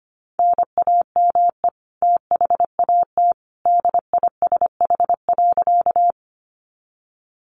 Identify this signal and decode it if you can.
Morse code, heard as a keyed tone, about 25 wpm, 700 Hz. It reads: NAME THAT DISH.
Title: Sentences from Top 1300 Words 25wpm